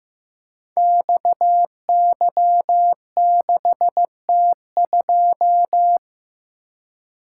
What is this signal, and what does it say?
Morse code, heard as a keyed tone, about 15 wpm, 700 Hz. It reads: XY6T2
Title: Code Groups - 5 Letter:Numbers - 15wpm